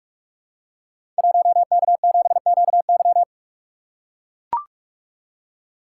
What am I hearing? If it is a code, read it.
Morse code, heard as a keyed tone, about 44 wpm, 700 Hz. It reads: 1K7QY
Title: Code Groups - 5 Letter:Numbers - 45wpm